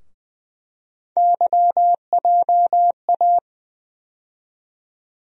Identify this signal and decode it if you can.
Morse code, heard as a keyed tone, about 20 wpm, 700 Hz. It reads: YJA
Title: USA Call Sign Postfixes - 3 Letters 20wpm